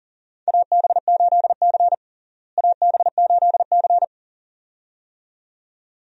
Morse code, heard as a keyed tone, about 40 wpm, 700 Hz. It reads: AB8C AB8C
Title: USA Call Signs - 4 Characters - Repeated 2x - 40wpm